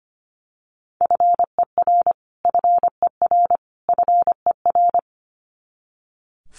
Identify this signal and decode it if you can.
Morse code, heard as a keyed tone, about 25 wpm, 700 Hz. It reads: FER FER FER
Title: Abbreviations Repeated 3x - 25wpm